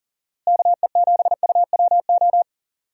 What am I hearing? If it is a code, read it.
Morse code, heard as a keyed tone, about 40 wpm, 700 Hz. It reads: KE7UWO
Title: USA Call Signs - 6 Characters - Repeated 3x - 40wpm